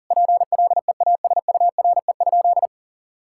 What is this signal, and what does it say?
Morse code, heard as a keyed tone, about 37 wpm, 700 Hz. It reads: PLEASURE?
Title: Sentences from Top 1600 Words 40wpm